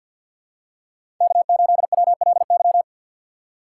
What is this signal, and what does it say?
Morse code, heard as a keyed tone, about 51 wpm, 700 Hz. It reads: K8WLY
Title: USA Call Signs - 3x Character Spacing - 50wpm